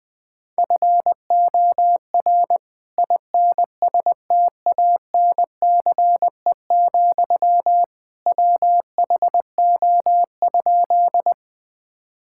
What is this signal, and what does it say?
Morse code, heard as a keyed tone, about 20 wpm, 700 Hz. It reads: FOR INSTANCE, WHO?
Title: Sentences from Top 1000 Words - Review Part 5 - 20wpm